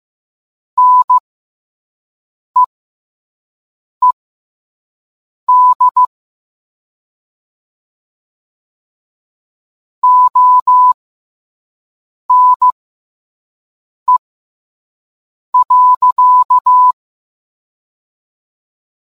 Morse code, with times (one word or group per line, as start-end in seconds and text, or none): 0.8-6.0 NEED
10.0-16.9 ONE.